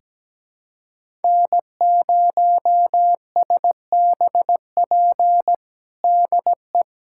Morse code, heard as a keyed tone, about 17 wpm, 700 Hz. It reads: N0SBP DE
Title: USA Callsign DE Callsign - 5 Characters - 17wpm